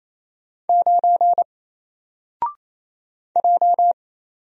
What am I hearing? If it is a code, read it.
Morse code, heard as a keyed tone, about 28 wpm, 700 Hz. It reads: 9 J
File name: Single-Letter-Number-28wpm